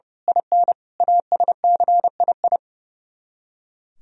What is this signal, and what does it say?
Morse code, heard as a keyed tone, about 29 wpm, 700 Hz. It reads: IN ASCII
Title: Speed-Racing - Sets of 5 Words 30wpm